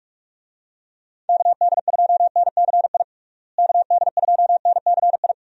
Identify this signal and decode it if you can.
Morse code, heard as a keyed tone, about 46 wpm, 700 Hz. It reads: KD2NCI KD2NCI
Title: USA Call Signs - 6 Characters - Repeated 2x - 45wpm